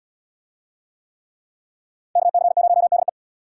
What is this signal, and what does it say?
Morse code, heard as a keyed tone, about 77 wpm, 690 Hz. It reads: KC0DE